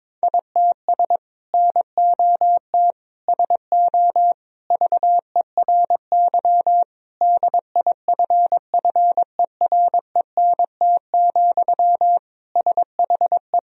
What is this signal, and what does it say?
Morse code, heard as a keyed tone, about 22 wpm, 700 Hz. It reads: ITS NOT SO VERY DIFFERENT, SHE
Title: Sentences from Top 1000 Words - Review Part 3 - 22wpm